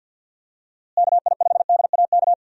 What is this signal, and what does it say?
Morse code, heard as a keyed tone, about 52 wpm, 700 Hz. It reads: KI5DAK
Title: USA Call Signs - 6 Characters - Repeated 3x - 50wpm